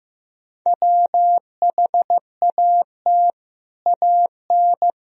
Morse code, heard as a keyed tone, about 16 wpm, 700 Hz. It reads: WHAT AN
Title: Sentences from Top 1000 Words - Review Part 3 - 15wpm